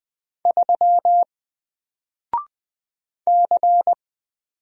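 Morse code, heard as a keyed tone, about 20 wpm, 700 Hz. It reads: 3 C
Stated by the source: Single letter and number Morse drill at 20 wpm, from N4LCD.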